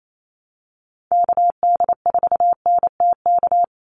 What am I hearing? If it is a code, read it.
Morse code, heard as a keyed tone, about 28 wpm, 700 Hz. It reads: KD4NTK
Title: USA Call Signs 28wpm